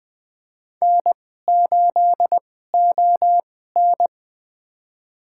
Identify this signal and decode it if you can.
Morse code, heard as a keyed tone, about 20 wpm, spaced as for 14 wpm, 700 Hz. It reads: N8ON
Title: USA Call Signs - 4 Characters - 2x Character Spacing - 20wpm